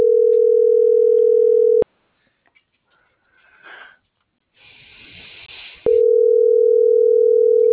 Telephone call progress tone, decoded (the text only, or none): ringback tone